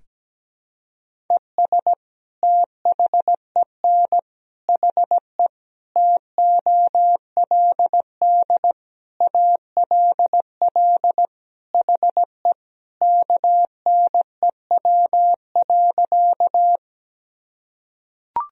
Morse code, heard as a keyed tone, about 17 wpm, 700 Hz. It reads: ES THEN HE TOLD ALL HE KNEW.
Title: Sentences from Top 400 Words and Abbreviations 17wpm